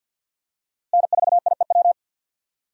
Morse code, heard as a keyed tone, about 47 wpm, 700 Hz. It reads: N4IEW
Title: USA Call Signs - 5 Characters - Repeated 2x - 50wpm